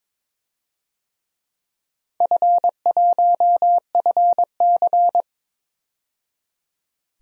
Morse code, heard as a keyed tone, about 21 wpm, 700 Hz. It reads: F1FC
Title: DXCC-100 Call Signs - 4 Characters - 22wpm